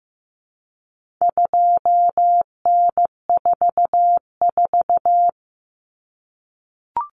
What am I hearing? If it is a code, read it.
Morse code, heard as a keyed tone, about 15 wpm, 700 Hz. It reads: 2N44
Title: Common Cut Numbers - 4-Digit 15wpm